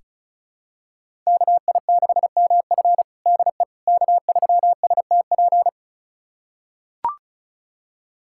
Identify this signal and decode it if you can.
Morse code, heard as a keyed tone, about 36 wpm, 700 Hz. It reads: KI6MF DE K3STP
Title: USA Callsign DE Callsign - 5 Characters - 35wpm